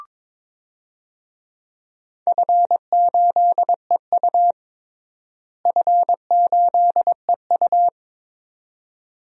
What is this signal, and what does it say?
Morse code, heard as a keyed tone, about 22 wpm, 700 Hz. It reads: F8EU F8EU
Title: DXCC-100 Call Signs - 4 Characters - Repeated 2x - 22wpm